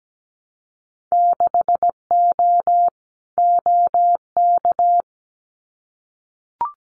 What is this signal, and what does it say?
Morse code, heard as a keyed tone, about 17 wpm, 700 Hz. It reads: 6O OK
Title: Winter Field Day Exchanges Encore 17wpm